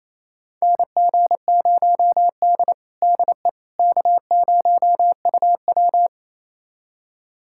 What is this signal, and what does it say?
Morse code, heard as a keyed tone, about 28 wpm, 700 Hz. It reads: NG0D DE K0UW
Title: USA Callsign DE Callsign - 4 Characters - 28wpm